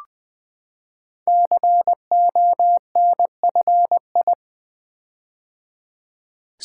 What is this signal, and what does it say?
Morse code, heard as a keyed tone, about 20 wpm, 700 Hz. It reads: CONFI
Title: Build A Word - 5 Letters At A Time - 10-Letter Words - 20wpm